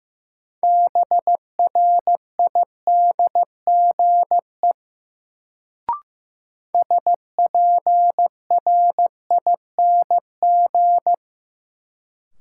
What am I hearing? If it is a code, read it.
Morse code, heard as a keyed tone, about 15 wpm, 700 Hz. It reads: BRIDGE SPRING
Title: Sentences from Top 1200 Words - New Word Drill - 15wpm